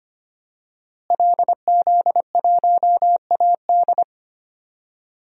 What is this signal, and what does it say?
Morse code, heard as a keyed tone, about 25 wpm, 700 Hz. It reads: LZ1AD
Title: DXCC-100 Call Signs - 5 Characters - Repeated 2x - 25wpm